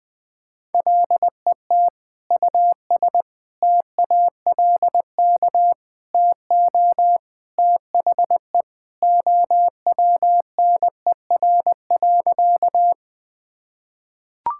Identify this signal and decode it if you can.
Morse code, heard as a keyed tone, about 20 wpm, 700 Hz. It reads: LET US TALK TO THE OWNER.
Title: Sentences from Top 1600 Words 20wpm